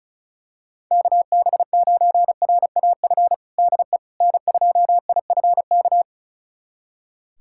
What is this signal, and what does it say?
Morse code, heard as a keyed tone, about 36 wpm, 700 Hz. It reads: KB9RAF DE N2IFK
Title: USA Callsign DE Callsign 35wpm